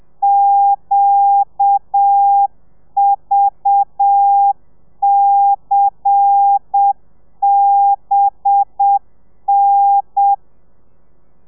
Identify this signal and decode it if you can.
Morse code, heard as a keyed tone, about 7 wpm, 800 Hz. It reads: QVCBN
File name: telegrafia / i2tfj / Lez10 30